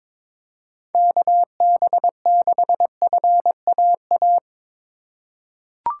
Morse code, heard as a keyed tone, about 21 wpm, 700 Hz. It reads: KB6FAA
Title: USA Call Signs - 6 Characters - 6x Character Spacing - 22wpm